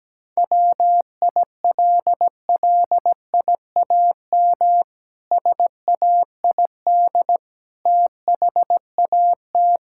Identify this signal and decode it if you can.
Morse code, heard as a keyed tone, about 17 wpm, 700 Hz. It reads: WILLIAM SAID THAT
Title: Pangram Challenge v2 - 17wpm